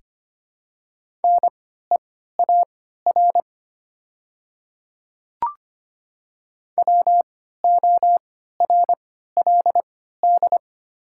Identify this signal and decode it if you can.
Morse code, heard as a keyed tone, about 25 wpm, spaced as for 14 wpm, 700 Hz. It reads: NEAR WORLD
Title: 3x Character Spacing - Top 500 Words - 25wpm